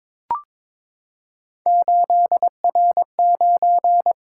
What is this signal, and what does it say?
Morse code, heard as a keyed tone, about 23 wpm, 700 Hz. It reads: 8R9
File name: practice / 22wpm / Three Letter-Number Combinations - 22wpm